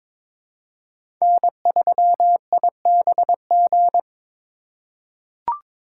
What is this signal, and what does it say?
Morse code, heard as a keyed tone, about 22 wpm, 700 Hz. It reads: N3IBG
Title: USA Call Signs - 5 Characters - 3x Character Spacing - 22wpm